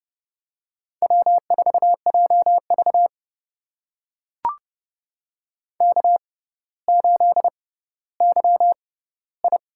Morse code, heard as a keyed tone, about 30 wpm, spaced as for 15 wpm, 700 Hz. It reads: W4JV K8YI
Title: USA Call Signs - 4 Characters - 6x Character Spacing - 30wpm